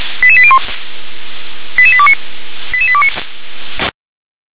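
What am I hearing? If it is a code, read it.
Five-tone selective call (ZVEI), heard as quick runs of five tones, tones of about 70 ms. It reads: 88801 88829 88829
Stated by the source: ZVEI five-tone selective call received on an SDR.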